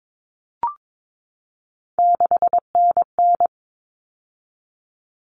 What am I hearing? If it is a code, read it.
Morse code, heard as a keyed tone, about 22 wpm, 700 Hz. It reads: 6NN
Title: Common Cut Numbers - 3-Digit 22wpm